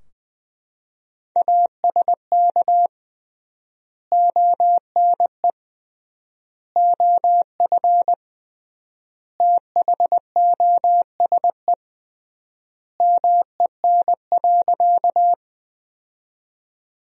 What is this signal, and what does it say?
Morse code, heard as a keyed tone, about 21 wpm, 700 Hz. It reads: ASK ONE OF THOSE MEN.
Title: 3x Word-Spacing - Sentences from Top 300 Words - 20wpm